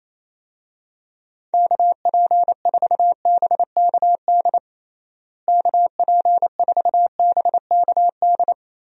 Morse code, heard as a keyed tone, about 28 wpm, 700 Hz. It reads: KP4BKD KP4BKD
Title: DXCC-100 Call Signs - 6 Characters - Repeated 3x - 28wpm